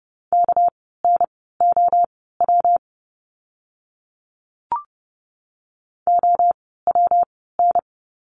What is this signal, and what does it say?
Morse code, heard as a keyed tone, about 30 wpm, spaced as for 17 wpm, 700 Hz. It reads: KNOW OWN
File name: course / 30:10wpm / 237 Lesson 38f - Letter q - Words 30wpm